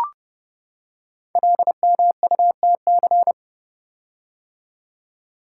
Morse code, heard as a keyed tone, about 29 wpm, 700 Hz. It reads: LMUTC
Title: Code Groups - 5 Letters - 30wpm